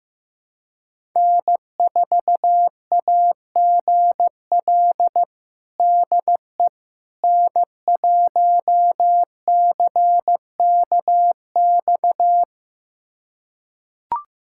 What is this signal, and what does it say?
Morse code, heard as a keyed tone, about 15 wpm, 700 Hz. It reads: N4AGL DE N1CKX
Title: USA Callsign DE Callsign - 5 Characters - 15wpm